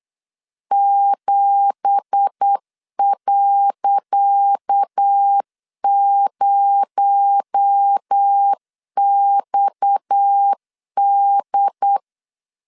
Morse code, heard as a keyed tone, about 8 wpm, 800 Hz. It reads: 7.0XD
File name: LEZIONI / 10 SIGNS / 10C DOT 8wpm 3649